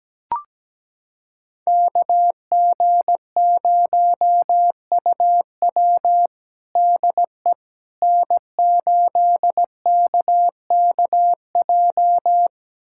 Morse code, heard as a keyed tone, about 17 wpm, 700 Hz. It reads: KG0UW DE N8KKJ
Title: USA Callsign DE Callsign - 5 Characters - 17wpm